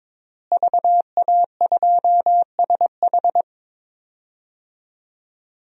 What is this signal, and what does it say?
Morse code, heard as a keyed tone, about 22 wpm, 700 Hz. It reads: VA2SH
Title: DXCC-100 Call Signs - 5 Characters - Repeated 3x - 22wpm